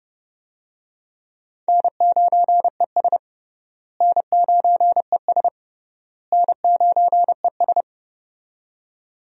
Morse code, heard as a keyed tone, about 30 wpm, 700 Hz. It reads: N9ES N9ES N9ES
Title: USA Call Signs - 4 Characters - Repeated 3x - 30wpm